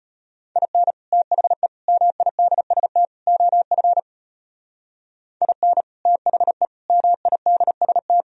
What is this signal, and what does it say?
Morse code, heard as a keyed tone, about 33 wpm, 700 Hz. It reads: IN THE MIDST OF IN THE MIDST